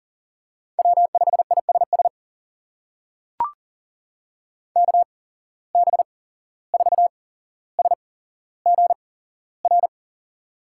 Morse code, heard as a keyed tone, about 40 wpm, spaced as for 16 wpm, 700 Hz. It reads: W5ISS KB4SGR
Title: USA Call Signs - 8x Character Spacing - 40wpm